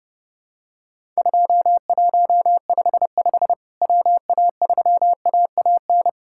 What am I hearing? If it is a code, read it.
Morse code, heard as a keyed tone, about 30 wpm, 700 Hz. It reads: 2155 WA3AAN
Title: Worked All Europe DX Contest - QTC Exchange - 30wpm